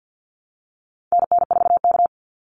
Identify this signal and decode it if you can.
Morse code, heard as a keyed tone, about 50 wpm, 700 Hz. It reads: NN4K
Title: Call Signs Repeated 2x - 50wpm